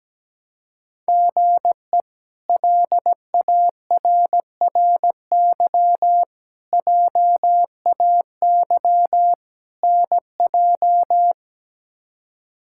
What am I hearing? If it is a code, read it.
Morse code, heard as a keyed tone, about 17 wpm, 700 Hz. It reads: GE LARRY JAY NJ
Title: K1USN SST QSO Practice - 17wpm